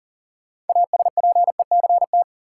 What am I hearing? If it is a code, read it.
Morse code, heard as a keyed tone, about 40 wpm, 700 Hz. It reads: ASPECT
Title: Sentences from Top 1300 Words - New Word Drill Alt - 40wpm